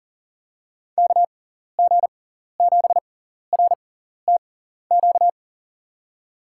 Morse code, heard as a keyed tone, about 40 wpm, spaced as for 14 wpm, 700 Hz. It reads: KG7RTQ
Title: USA Call Signs - 6 Characters - 6x Character Spacing - 40wpm